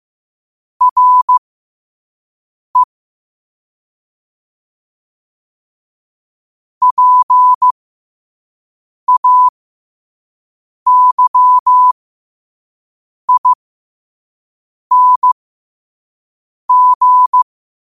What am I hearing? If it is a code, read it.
Morse code, heard as a keyed tone, about 15 wpm, spaced as for 5 wpm, 1000 Hz. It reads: RE PAYING
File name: qotdfiles / 5wpm